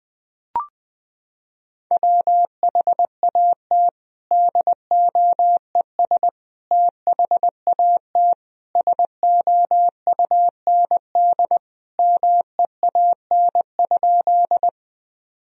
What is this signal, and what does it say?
Morse code, heard as a keyed tone, about 20 wpm, 700 Hz. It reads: WHAT DOES THAT SOUND MEAN?
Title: Sentences from Top 1000 Words - Review Part 1 - 20wpm